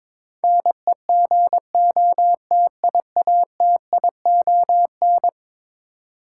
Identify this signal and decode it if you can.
Morse code, heard as a keyed tone, about 21 wpm, 700 Hz. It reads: NEGOTIATION
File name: practice / 22wpm / Eleven Letter Words - 22wpm